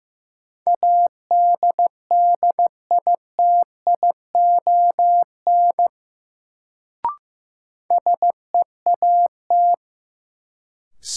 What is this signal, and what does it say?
Morse code, heard as a keyed tone, about 15 wpm, 700 Hz. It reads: ADDITION SEAT
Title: Sentences from Top 1500 Words - New Word Drill Alt - 15wpm